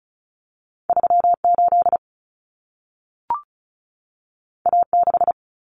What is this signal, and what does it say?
Morse code, heard as a keyed tone, about 36 wpm, 700 Hz. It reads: 38 A6
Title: Common Cut Numbers - 2-Digit 35wpm